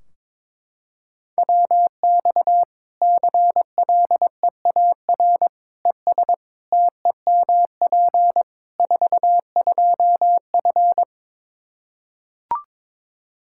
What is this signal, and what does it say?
Morse code, heard as a keyed tone, about 22 wpm, 700 Hz. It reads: WX CLEAR ES TEMP 42F
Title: Practice QSO - Exchange 2 - Info - 22wpm